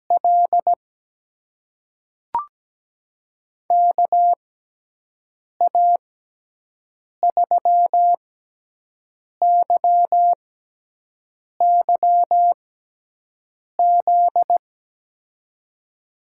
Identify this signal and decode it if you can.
Morse code, heard as a keyed tone, about 17 wpm, spaced as for 6 wpm, 700 Hz. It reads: L KA3YYZ
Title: USA Call Signs - 6 Characters - 6x Character Spacing - 17wpm